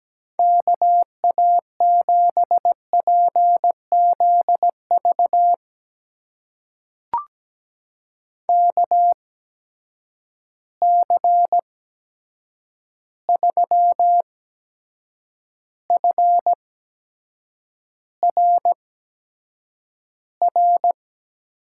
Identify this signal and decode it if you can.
Morse code, heard as a keyed tone, about 17 wpm, spaced as for 7 wpm, 700 Hz. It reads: KA7PZV KC3FRR
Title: USA Call Signs - 6 Characters - 8x Character Spacing - 17wpm